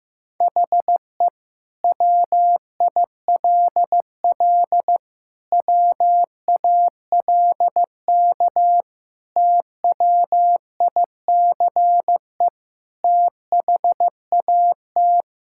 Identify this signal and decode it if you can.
Morse code, heard as a keyed tone, about 15 wpm, 700 Hz. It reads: HE WILL WALK TWICE THAT
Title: Sentences from Top 1600 Words 15wpm